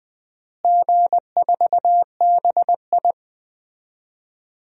Morse code, heard as a keyed tone, about 20 wpm, 700 Hz. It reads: G4BI
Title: DXCC-100 Call Signs - 4 Characters - Repeated 2x - 20wpm